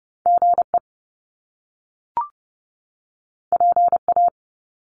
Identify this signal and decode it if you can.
Morse code, heard as a keyed tone, about 30 wpm, 700 Hz. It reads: GE PA
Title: Top Two Letter Combinations in Words 30wpm